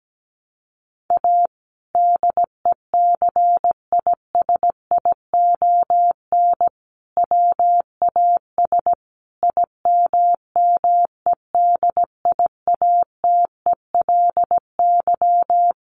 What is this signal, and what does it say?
Morse code, heard as a keyed tone, about 17 wpm, 700 Hz. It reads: A DECISION WAS IMMEDIATELY
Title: Sentences from Top 1100 Words 17wpm